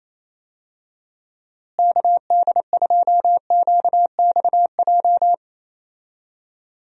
Morse code, heard as a keyed tone, about 28 wpm, 700 Hz. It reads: KD2QXJ
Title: USA Call Signs - 6 Characters - Repeated 2x - 28wpm